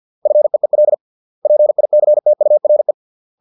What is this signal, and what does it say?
Morse code, heard as a keyed tone, about 50 wpm, 600 Hz. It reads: FEEL PICTURE